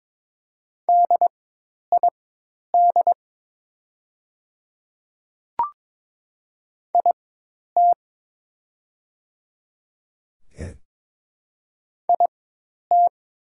Morse code, heard as a keyed tone, about 22 wpm, spaced as for 10 wpm, 700 Hz. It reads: DID IT IT
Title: 4x Character Spacing - Top 200 Words - 22wpm